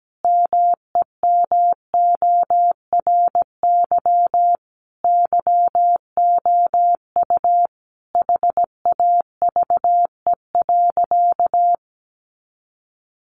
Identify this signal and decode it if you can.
Morse code, heard as a keyed tone, about 17 wpm, 700 Hz. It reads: MEMORY YOU HAVE.
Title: Sentences from Top 1300 Words 17wpm